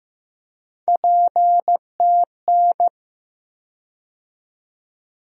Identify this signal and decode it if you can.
Morse code, heard as a keyed tone, about 15 wpm, 700 Hz. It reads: PTN
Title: USA Call Sign Postfixes - 3 Letters 15wpm